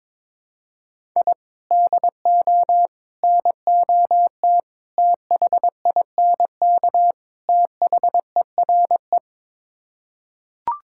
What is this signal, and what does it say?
Morse code, heard as a keyed tone, about 22 wpm, 700 Hz. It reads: I DO NOT THINK THERE
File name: practice / 22wpm / Build A Phrase - 5-Word Phrases - 22wpm